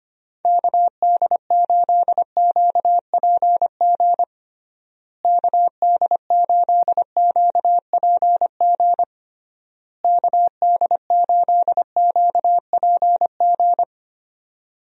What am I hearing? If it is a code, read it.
Morse code, heard as a keyed tone, about 25 wpm, 700 Hz. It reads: KD8QPG KD8QPG KD8QPG
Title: USA Call Signs - 6 Characters - Repeated 3x - 25wpm